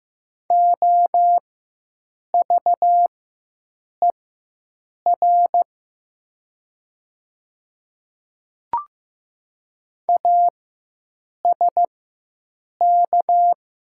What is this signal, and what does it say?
Morse code, heard as a keyed tone, about 15 wpm, spaced as for 7 wpm, 700 Hz. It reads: OVER ASK
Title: 4x Character Spacing - Top 200 Words - 15wpm